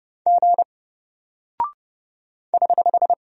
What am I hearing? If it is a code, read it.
Morse code, heard as a keyed tone, about 30 wpm, 700 Hz. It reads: G <HH>
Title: Single Character (Letter:Number:Punctuation:Prosign) Alt 30wpm